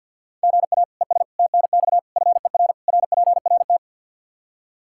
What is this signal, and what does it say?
Morse code, heard as a keyed tone, about 50 wpm, 700 Hz. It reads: GA ES TNX FER RPRT
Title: Practice QSO - Exchange 1 - Beginning - 50wpm